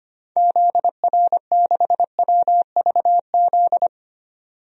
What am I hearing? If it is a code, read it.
Morse code, heard as a keyed tone, about 25 wpm, 700 Hz. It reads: ZR6WVZ
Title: DXCC-100 Call Signs - 6 Characters - 25wpm